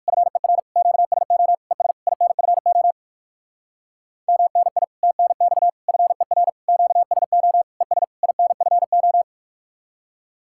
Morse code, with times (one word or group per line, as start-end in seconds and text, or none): 0.1-0.6 FER
0.8-1.5 QSO
1.7-1.9 ES
2.1-2.9 INFO
4.3-4.8 MNI
5.0-5.7 TNX
5.9-6.5 FER
6.7-7.6 QSO
7.8-8.0 ES
8.2-9.2 INFO